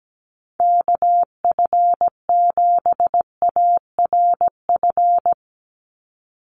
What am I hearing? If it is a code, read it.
Morse code, heard as a keyed tone, about 17 wpm, 700 Hz. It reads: KF7ARF
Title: USA Call Signs 17wpm